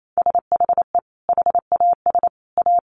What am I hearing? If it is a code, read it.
Morse code, heard as a keyed tone, about 28 wpm, 700 Hz. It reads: SHE HAS A
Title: Sentences from Top 1400 Words 28wpm